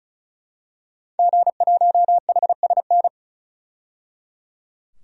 Morse code, heard as a keyed tone, about 36 wpm, 700 Hz. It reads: G1HSN
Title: DXCC-100 Call Signs - 5 Characters - 35wpm